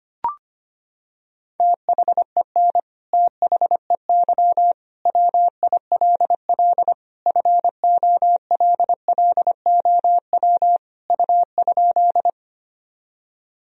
Morse code, heard as a keyed tone, about 25 wpm, 700 Hz. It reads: THEN THEY WILL FOLLOW U?
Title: Sentences from Top 200 Words and Abbreviations 25wpm